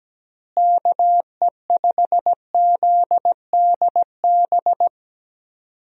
Morse code, heard as a keyed tone, about 17 wpm, 700 Hz. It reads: KE5ZDB
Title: USA Call Signs - 6 Characters - Repeated 2x - 17wpm